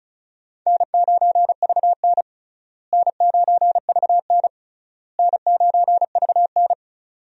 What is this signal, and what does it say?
Morse code, heard as a keyed tone, about 36 wpm, 700 Hz. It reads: N9VN N9VN N9VN
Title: USA Call Signs - 4 Characters - Repeated 3x - 35wpm